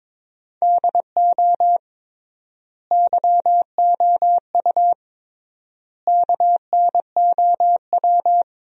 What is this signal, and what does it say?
Morse code, heard as a keyed tone, about 22 wpm, 700 Hz. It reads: DO YOU KNOW
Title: 3x Word-Spacing - Sentences from Top 100 Words - 22wpm